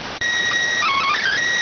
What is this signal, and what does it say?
SSTV (slow-scan television), PD 290.